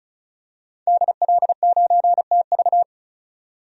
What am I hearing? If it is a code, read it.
Morse code, heard as a keyed tone, about 34 wpm, 700 Hz. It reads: DL9TV
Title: DXCC-100 Call Signs - 5 Characters - 35wpm